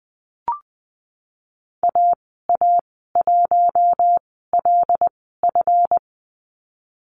Morse code, heard as a keyed tone, about 20 wpm, spaced as for 15 wpm, 700 Hz. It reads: AA1LF
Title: 186 Lesson 30g - Letter b - Call Signs 20wpm